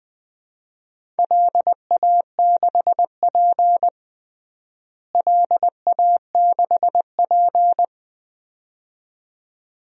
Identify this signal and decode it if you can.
Morse code, heard as a keyed tone, about 21 wpm, 700 Hz. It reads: LA6P LA6P
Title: DXCC-100 Call Signs - 4 Characters - Repeated 2x - 20wpm